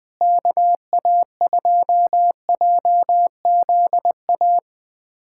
Morse code, heard as a keyed tone, about 21 wpm, 700 Hz. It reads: KA2JZA
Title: USA Callsign DE Callsign 20wpm